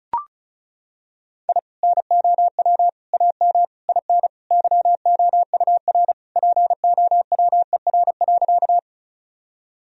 Morse code, heard as a keyed tone, about 36 wpm, 700 Hz. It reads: I NOW AM IN YOUR POWER.
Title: Sentences from Top 1000 Words - Review Part 2 - 35wpm